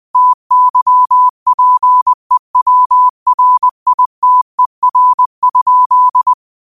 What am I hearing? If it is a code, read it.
Morse code, heard as a keyed tone, about 20 wpm, 1000 Hz. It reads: TYPEWRITER?
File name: qotdfiles / 20wpm